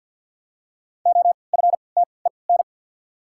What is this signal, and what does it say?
Morse code, heard as a keyed tone, about 48 wpm, 700 Hz. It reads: O F T E N
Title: 3x Character Spacing - Top 300 Words - 50wpm